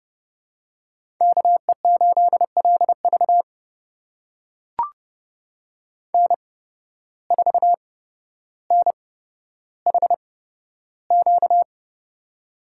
Morse code, heard as a keyed tone, about 28 wpm, spaced as for 13 wpm, 700 Hz. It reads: KE8LV N4NHQ
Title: USA Call Signs - 5 Characters - 8x Character Spacing - 30wpm